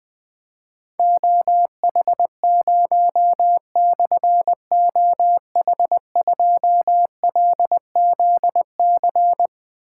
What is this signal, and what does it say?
Morse code, heard as a keyed tone, about 20 wpm, 700 Hz. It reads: OH0/OH2LZC
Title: Difficult Call Signs 20wpm